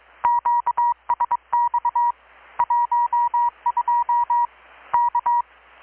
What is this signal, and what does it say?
Morse code, heard as a keyed tone, about 23 wpm, 960 Hz. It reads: QSX 12 K
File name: blog / swl / 6-abril-2019-12695-USB-KPS